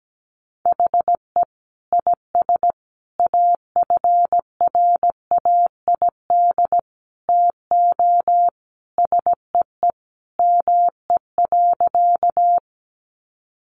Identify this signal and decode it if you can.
Morse code, heard as a keyed tone, about 17 wpm, 700 Hz. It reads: HE IS AFRAID TO SEE ME.
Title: Sentences from Top 1500 Words 17wpm